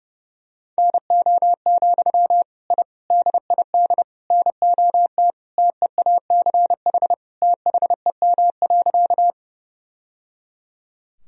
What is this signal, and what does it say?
Morse code, heard as a keyed tone, about 32 wpm, 700 Hz. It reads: NO, I DID NOT TEACH THEM.
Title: Sentences from Top 1000 Words - Review Part 2 - 30wpm